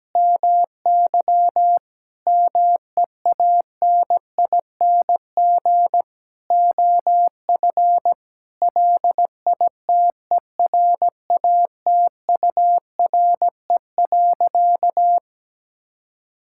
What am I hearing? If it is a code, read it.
Morse code, heard as a keyed tone, about 17 wpm, 700 Hz. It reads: MY MEANING OF LITERATURE.
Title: Sentences from Top 1000 Words - Review Part 6 - 17wpm